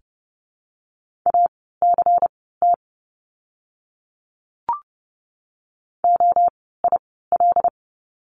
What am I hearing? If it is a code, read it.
Morse code, heard as a keyed tone, about 30 wpm, spaced as for 17 wpm, 700 Hz. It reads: ACT OIL
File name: Lesson 16f - Letter c - Words 30wpm